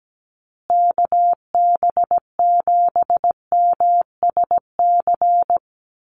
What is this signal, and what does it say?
Morse code, heard as a keyed tone, about 17 wpm, 700 Hz. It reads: KB7MSC
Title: USA Call Signs Repeated 2x 17wpm